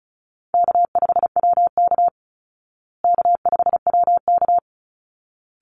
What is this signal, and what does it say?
Morse code, heard as a keyed tone, about 34 wpm, 700 Hz. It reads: K5WK K5WK
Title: Easy Call Signs Repeated 2x - 35wpm